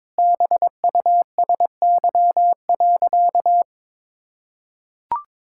Morse code, heard as a keyed tone, about 22 wpm, 700 Hz. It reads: BUSY.
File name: Sentences with 7-Word Phrases - 22wpm